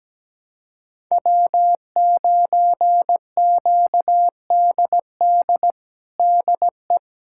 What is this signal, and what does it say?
Morse code, heard as a keyed tone, about 17 wpm, 700 Hz. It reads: W9QDD DE
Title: USA Callsign DE Callsign 17wpm